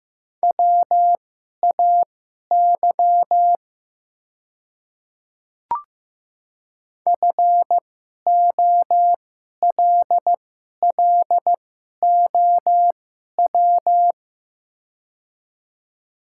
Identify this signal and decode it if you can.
Morse code, heard as a keyed tone, about 15 wpm, spaced as for 11 wpm, 700 Hz. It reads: WAY FOLLOW
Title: 2x Character Spacing - Top 200 Words - 15wpm